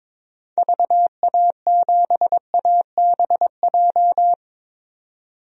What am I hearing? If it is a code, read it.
Morse code, heard as a keyed tone, about 23 wpm, 700 Hz. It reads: VA7ABJ
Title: DXCC-100 Call Signs - 6 Characters - 22wpm